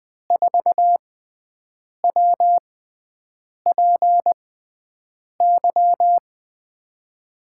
Morse code, heard as a keyed tone, about 21 wpm, spaced as for 7 wpm, 700 Hz. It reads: 4WPY
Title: USA Call Signs - 6 Characters - 6x Character Spacing - 20wpm